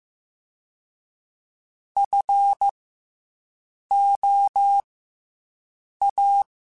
Morse code, heard as a keyed tone, about 15 wpm, spaced as for 6 wpm, 800 Hz. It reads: FOA